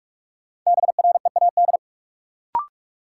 Morse code, heard as a keyed tone, about 46 wpm, 700 Hz. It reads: BREAD
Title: Sentences from Top 1600 Words - New Word Drill Alt - 45wpm